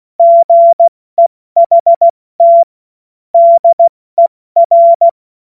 Morse code, heard as a keyed tone, about 16 wpm, 680 Hz. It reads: GEHT DER
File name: Witz-08-Juli-2025-70-F80